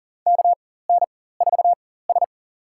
Morse code, heard as a keyed tone, about 40 wpm, spaced as for 19 wpm, 700 Hz. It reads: KN4S
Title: USA Call Signs - 4 Characters - 4x Character Spacing - 40wpm